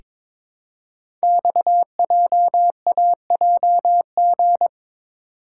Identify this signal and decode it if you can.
Morse code, heard as a keyed tone, about 22 wpm, 700 Hz. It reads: XJAJG
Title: Code Groups - 5 Letters - 22wpm